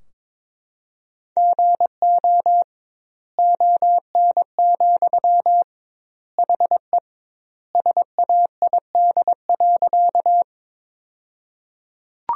Morse code, heard as a keyed tone, about 22 wpm, 700 Hz. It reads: GO ON, HE SAID.
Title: 2x Word-Spacing - Sentences from Top 100 Words - 22wpm